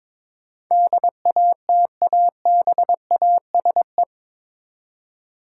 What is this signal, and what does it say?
Morse code, heard as a keyed tone, about 22 wpm, 700 Hz. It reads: DATABASE
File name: practice / 22wpm / Eight Letter Words - 22wpm